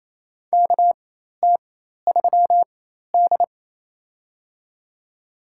Morse code, heard as a keyed tone, about 28 wpm, spaced as for 13 wpm, 700 Hz. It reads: KT3D
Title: USA Call Signs - 4x Character Spacing - 28wpm